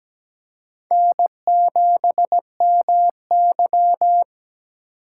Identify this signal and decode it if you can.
Morse code, heard as a keyed tone, about 17 wpm, 700 Hz. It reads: N7MY
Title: USA Call Signs - 4 Characters - 4x Character Spacing - 17wpm